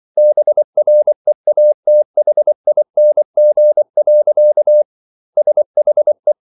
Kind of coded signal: Morse code